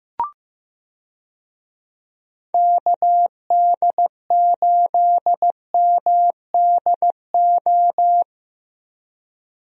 Morse code, heard as a keyed tone, about 15 wpm, 700 Hz. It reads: KD8MDO
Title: USA Call Signs - 6 Characters - 15wpm